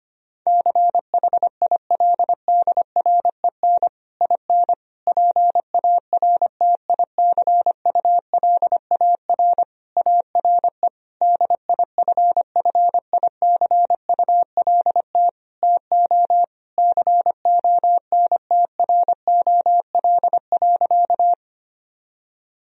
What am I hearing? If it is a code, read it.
Morse code, heard as a keyed tone, about 25 wpm, 700 Hz. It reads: CHILDREN IN PARTICULAR ARE DIFFICULT TO CONTROL.
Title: Sentences from Top 1000 Words - Review Part 4 - 25wpm